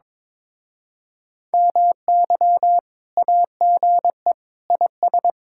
Morse code, heard as a keyed tone, about 22 wpm, 700 Hz. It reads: MY AGE IS
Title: Speed-Racing Common QSO Phrases - 22wpm